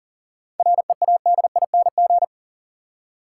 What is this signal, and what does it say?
Morse code, heard as a keyed tone, about 40 wpm, 700 Hz. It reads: READING
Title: Seven Letter Words - 40wpm